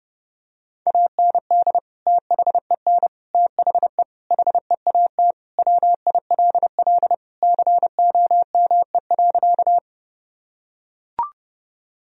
Morse code, heard as a keyed tone, about 32 wpm, 700 Hz. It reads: AND THEN THE HEAT WILL COME.
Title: Sentences from Top 1000 Words - Review Part 3 - 30wpm